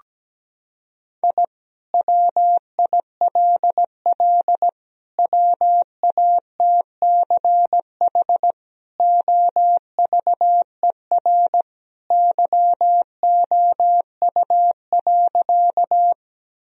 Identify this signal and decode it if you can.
Morse code, heard as a keyed tone, about 17 wpm, 700 Hz. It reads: I WILL WATCH OVER YOU.
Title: Sentences from Top 1000 Words - Review Part 1 - 17wpm